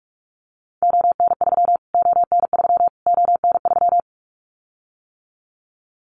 Morse code, heard as a keyed tone, about 46 wpm, 700 Hz. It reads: ON3 ON3 ON3
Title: Callsign Prefixes for Top 25 DXCC - Repeated 3x - 45wpm